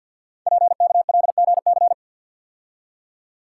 Morse code, heard as a keyed tone, about 52 wpm, 700 Hz. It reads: PKLGC
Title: Code Groups - 5 Letter:Numbers - 50wpm